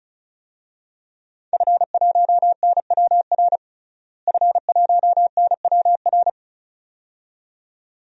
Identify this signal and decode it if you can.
Morse code, heard as a keyed tone, about 36 wpm, 700 Hz. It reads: F1NWR F1NWR
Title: DXCC-100 Call Signs - 5 Characters - Repeated 2x - 35wpm